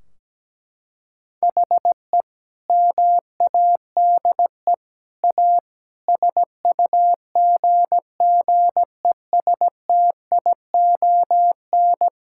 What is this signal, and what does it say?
Morse code, heard as a keyed tone, about 17 wpm, 700 Hz. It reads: HE MADE A SUGGESTION
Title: Sentences from Top 1600 Words 17wpm